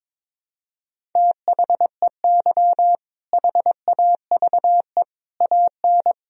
Morse code, heard as a keyed tone, about 22 wpm, 700 Hz. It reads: THEY HAVE AN
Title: Sentences from Top 1000 Words - Review Part 4 - 22wpm